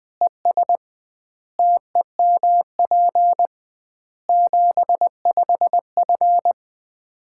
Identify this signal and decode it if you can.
Morse code, heard as a keyed tone, about 20 wpm, 700 Hz. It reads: ES TEMP 75F
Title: Common QSO Phrases - 2x Word-Spacing - 20wpm